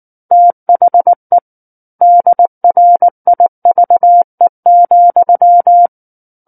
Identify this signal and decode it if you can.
Morse code, heard as a keyed tone, about 19 wpm, 700 Hz. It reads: THE DRIVE,
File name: Book One / 10-Chapter 10 - 19 WPM